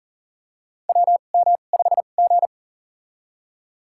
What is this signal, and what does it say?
Morse code, heard as a keyed tone, about 40 wpm, spaced as for 29 wpm, 700 Hz. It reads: WM5G
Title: USA Call Signs - 4 Characters - 2x Character Spacing - 40wpm